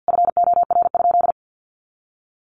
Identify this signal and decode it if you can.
Morse code, heard as a keyed tone, about 50 wpm, 700 Hz. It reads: FOR?
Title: Sentences from Top 1300 Words 50wpm